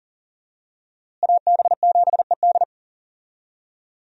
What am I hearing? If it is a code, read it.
Morse code, heard as a keyed tone, about 40 wpm, 700 Hz. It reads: AB7ED
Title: USA Call Signs - 5 Characters - Repeated 3x - 40wpm